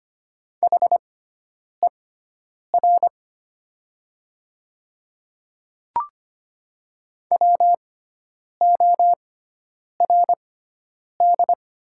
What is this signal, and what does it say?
Morse code, heard as a keyed tone, about 25 wpm, spaced as for 9 wpm, 690 Hz. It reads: HER WORD
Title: 6x Character Spacing - Top 100 Words - 25wpm